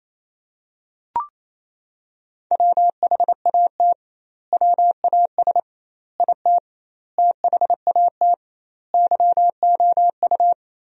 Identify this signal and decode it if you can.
Morse code, heard as a keyed tone, about 28 wpm, 700 Hz. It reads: WHAT WAS IT THAT YOU
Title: 2x Word-Spacing - Sentences from Top 300 Words - 28wpm